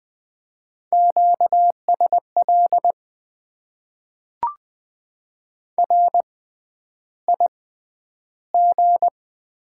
Morse code, heard as a keyed tone, about 20 wpm, spaced as for 10 wpm, 700 Hz. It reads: QSL RIG
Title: Top 90 QSO Elements - 6x Character Spacing - 20wpm